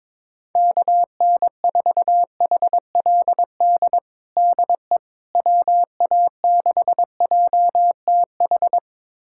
Morse code, heard as a keyed tone, about 22 wpm, 700 Hz. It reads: KN4HLD DE WA6JTH
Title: USA Callsign DE Callsign 22wpm